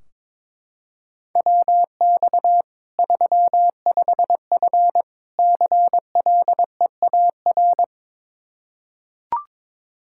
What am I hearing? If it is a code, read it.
Morse code, heard as a keyed tone, about 22 wpm, 700 Hz. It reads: WX 35F CLEAR
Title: K1USN SST QSO Practice 22wpm